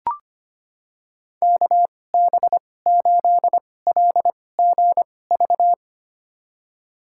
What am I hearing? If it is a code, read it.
Morse code, heard as a keyed tone, about 25 wpm, spaced as for 18 wpm, 700 Hz. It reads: KB8LGV
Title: USA Call Signs - 6 Characters - 2x Character Spacing - 25wpm